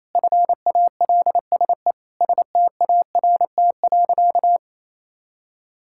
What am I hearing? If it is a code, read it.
Morse code, heard as a keyed tone, about 28 wpm, 700 Hz. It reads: FALSE START.
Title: Sentences from Top 1600 Words 28wpm